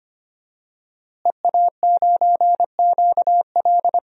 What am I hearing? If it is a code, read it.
Morse code, heard as a keyed tone, about 25 wpm, 700 Hz. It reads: EA9QL
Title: DXCC-100 Call Signs - 5 Characters - Repeated 3x - 25wpm